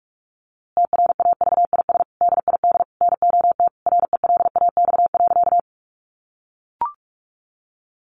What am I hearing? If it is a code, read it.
Morse code, heard as a keyed tone, about 44 wpm, 700 Hz. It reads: TRAVIS DID NOT RELAX.
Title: A Sound Of Thunder 45wpm